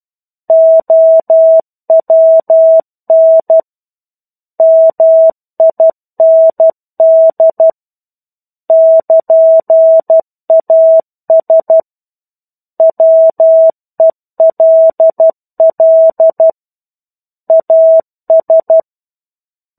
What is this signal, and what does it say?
Morse code, heard as a keyed tone, about 12 wpm, 650 Hz. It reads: OWN MIND <KN>AS WELL AS
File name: 01-Chapter-01 12 WPM